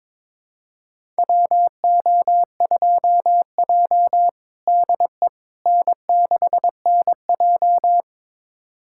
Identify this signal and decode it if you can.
Morse code, heard as a keyed tone, about 22 wpm, 700 Hz. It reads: WO2J DE N6NJ
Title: USA Callsign DE Callsign - 4 Characters - 22wpm